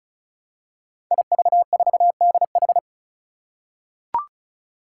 Morse code, heard as a keyed tone, about 36 wpm, 700 Hz. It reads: IV4DH
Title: Five Challenging Letter-Number Combinations 35wpm